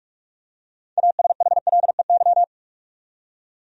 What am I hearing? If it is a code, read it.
Morse code, heard as a keyed tone, about 45 wpm, 700 Hz. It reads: ASHLEY
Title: USA Common Names - 45wpm